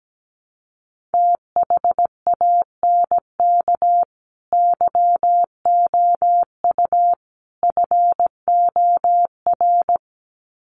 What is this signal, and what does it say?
Morse code, heard as a keyed tone, about 17 wpm, 700 Hz. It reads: THANK YOU FOR
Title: Sets of 3 Words 17wpm